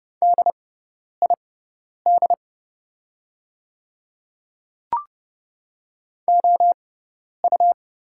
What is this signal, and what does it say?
Morse code, heard as a keyed tone, about 30 wpm, spaced as for 10 wpm, 700 Hz. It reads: DID OU
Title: 6x Character Spacing - Top 200 Words - 30wpm